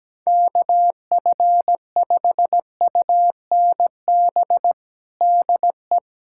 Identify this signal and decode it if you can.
Morse code, heard as a keyed tone, about 17 wpm, 700 Hz. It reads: KF5UNB DE
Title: USA Callsign DE Callsign - 6 Characters - 17wpm